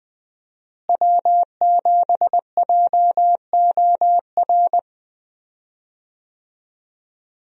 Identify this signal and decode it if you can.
Morse code, heard as a keyed tone, about 20 wpm, 700 Hz. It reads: W7JOR
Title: USA Call Signs - 5 Characters - Repeated 2x - 20wpm